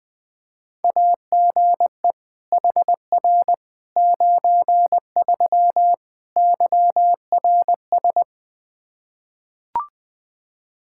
Morse code, heard as a keyed tone, about 20 wpm, 700 Hz. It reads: AGE HR 93 YRS
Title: Practice QSO - Exchange 3 - More Info - 20wpm